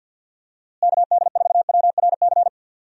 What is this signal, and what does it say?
Morse code, heard as a keyed tone, about 46 wpm, 700 Hz. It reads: KD4WRC
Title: USA Call Signs - 6 Characters - Repeated 3x - 50wpm